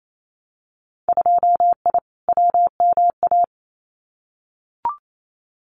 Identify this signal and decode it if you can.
Morse code, heard as a keyed tone, about 28 wpm, 700 Hz. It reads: 2I WMA
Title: Winter Field Day Exchanges Encore 28wpm